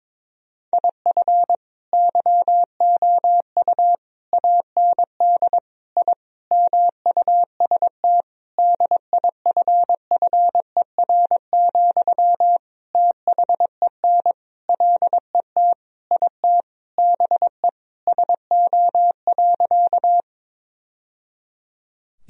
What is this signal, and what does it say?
Morse code, heard as a keyed tone, about 22 wpm, 700 Hz. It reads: IF YOU AND I MUST DIFFER, THEN LET IT BE SO.
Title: Sentences from Top 1000 Words - Review Part 1 - 22wpm